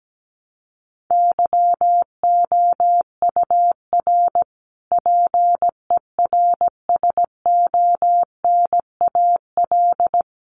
Morse code, heard as a keyed tone, about 17 wpm, 700 Hz. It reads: YOUR PERSONAL
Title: A Sound Of Thunder 17wpm